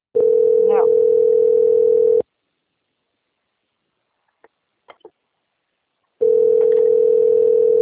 Ringback tone (North America), 440 + 480 Hz.
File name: out- - - - - .34655